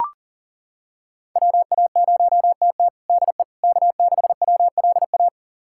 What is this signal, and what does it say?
Morse code, heard as a keyed tone, about 40 wpm, 700 Hz. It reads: WA0TT DE K6WLA
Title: USA Callsign DE Callsign - 5 Characters - 40wpm